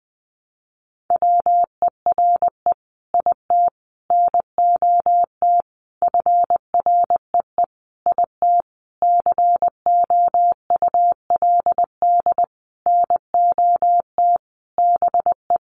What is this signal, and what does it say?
Morse code, heard as a keyed tone, about 20 wpm, 700 Hz. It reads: WERE IT NOT FREE IT COULD NOT BE
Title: Sentences from Top 900 Words 20wpm